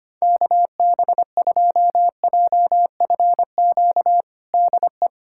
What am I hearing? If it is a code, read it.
Morse code, heard as a keyed tone, about 25 wpm, 700 Hz. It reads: KB2JFQ DE
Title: USA Callsign DE Callsign - 6 Characters - 25wpm